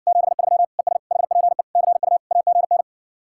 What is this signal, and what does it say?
Morse code, heard as a keyed tone, about 60 wpm, 700 Hz. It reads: 73 ES HPE CU AGN